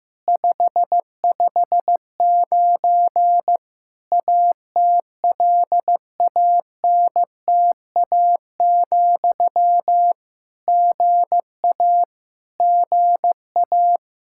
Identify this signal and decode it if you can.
Morse code, heard as a keyed tone, about 15 wpm, 700 Hz. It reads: 559 ATLANTA, GA GA
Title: K1USN SST QSO Practice 15wpm